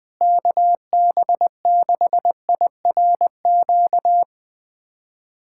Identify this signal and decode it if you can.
Morse code, heard as a keyed tone, about 20 wpm, 700 Hz. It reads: KB6IRQ
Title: USA Callsign DE Callsign 20wpm